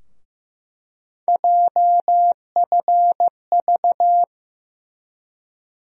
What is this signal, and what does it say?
Morse code, heard as a keyed tone, about 16 wpm, 700 Hz. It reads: JFV
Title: USA Call Sign Postfixes - 3 Letters 15wpm